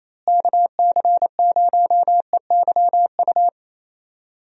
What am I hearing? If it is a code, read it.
Morse code, heard as a keyed tone, about 28 wpm, 700 Hz. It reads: KC0EYU
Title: USA Call Signs - 6 Characters - Repeated 3x - 28wpm